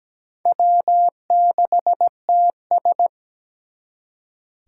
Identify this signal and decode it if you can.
Morse code, heard as a keyed tone, about 20 wpm, 700 Hz. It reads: W6TS
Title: Speed Racing - USA 1x2 Call Signs 17wpm